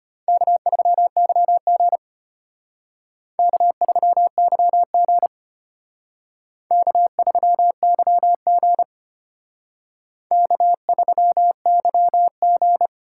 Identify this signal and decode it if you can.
Morse code, heard as a keyed tone, about 32 wpm, 700 Hz. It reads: K3YG K3YG K3YG K3YG